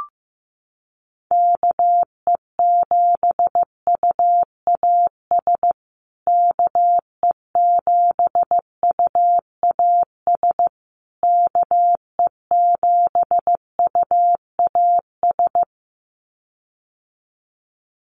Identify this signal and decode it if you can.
Morse code, heard as a keyed tone, about 14 wpm, 700 Hz. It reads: KE7UAS KE7UAS KE7UAS
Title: USA Call Signs Repeated 3x 15wpm